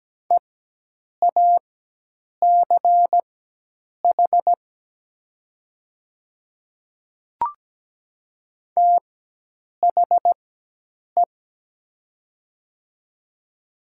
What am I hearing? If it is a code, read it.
Morse code, heard as a keyed tone, about 17 wpm, spaced as for 8 wpm, 700 Hz. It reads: EACH THE
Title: 4x Character Spacing - Top 100 Words - 17wpm